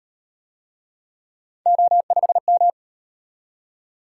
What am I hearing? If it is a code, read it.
Morse code, heard as a keyed tone, about 37 wpm, 700 Hz. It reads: O5M